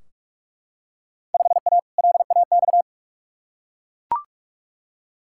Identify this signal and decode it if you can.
Morse code, heard as a keyed tone, about 43 wpm, 700 Hz. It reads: 5A LAX
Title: Field Day Exchanges - 45wpm